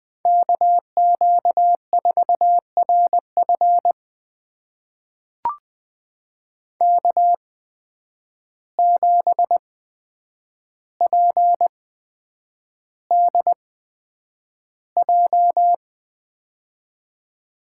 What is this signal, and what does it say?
Morse code, heard as a keyed tone, about 20 wpm, spaced as for 9 wpm, 700 Hz. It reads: KQ4RF K7PDJ
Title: USA Call Signs - 5 Characters - 8x Character Spacing - 20wpm